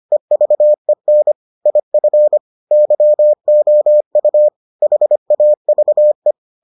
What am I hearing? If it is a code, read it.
Morse code, heard as a keyed tone, about 25 wpm, 600 Hz. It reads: EVEN IF YOU HAVE